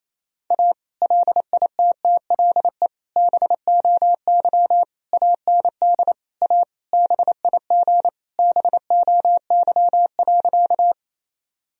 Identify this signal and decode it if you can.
Morse code, heard as a keyed tone, about 28 wpm, 700 Hz. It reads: A LITTLE BOY AND A BIG BOY.
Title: Sentences from Top 1000 Words - Review Part 1 - 28wpm